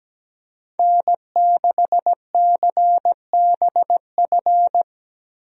Morse code, heard as a keyed tone, about 17 wpm, 700 Hz. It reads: N6CBF
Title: USA Call Signs - 5 Characters - Repeated 2x - 17wpm